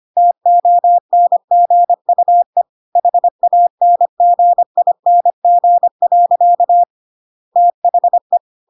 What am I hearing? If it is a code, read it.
Morse code, heard as a keyed tone, about 25 wpm, 700 Hz. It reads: TONGUE HANGING. THE